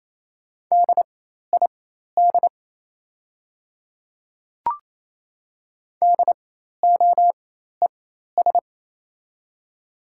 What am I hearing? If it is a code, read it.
Morse code, heard as a keyed tone, about 28 wpm, spaced as for 13 wpm, 700 Hz. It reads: DID DOES
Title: 4x Character Spacing - Top 500 Words - 28wpm